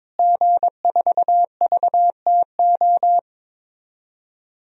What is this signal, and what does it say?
Morse code, heard as a keyed tone, about 22 wpm, 700 Hz. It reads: G4VTO
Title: DXCC-100 Call Signs - 5 Characters - Repeated 3x - 22wpm